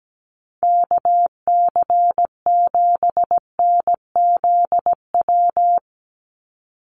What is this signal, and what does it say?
Morse code, heard as a keyed tone, about 17 wpm, 700 Hz. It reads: KC7NZW
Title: USA Call Signs 17wpm